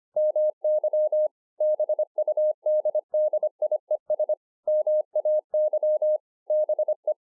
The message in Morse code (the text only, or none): MY BUDDIES MAY BE